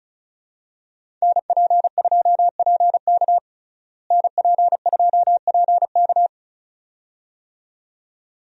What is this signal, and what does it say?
Morse code, heard as a keyed tone, about 34 wpm, 700 Hz. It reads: NP2PK NP2PK
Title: USA Call Signs - 5 Characters - Repeated 2x - 35wpm